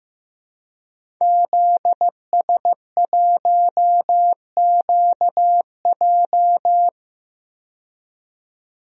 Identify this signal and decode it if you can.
Morse code, heard as a keyed tone, about 15 wpm, 700 Hz. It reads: ZS1QJ
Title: DXCC-100 Call Signs - 5 Characters - 15wpm